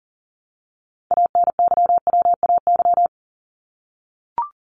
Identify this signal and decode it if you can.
Morse code, heard as a keyed tone, about 40 wpm, 700 Hz. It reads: ANYWAY
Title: Top 1501-1750 Words 40wpm